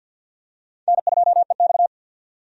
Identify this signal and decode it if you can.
Morse code, heard as a keyed tone, about 47 wpm, 700 Hz. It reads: N2EX
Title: USA Call Signs - 4 Characters - 50wpm